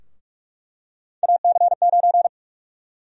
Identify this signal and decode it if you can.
Morse code, heard as a keyed tone, about 46 wpm, 700 Hz. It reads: AC9
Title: USA Call Sign Prefixes - Repeated 2x - 45wpm